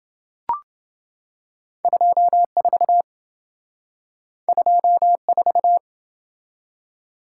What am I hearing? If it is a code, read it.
Morse code, heard as a keyed tone, about 27 wpm, 700 Hz. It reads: 24 24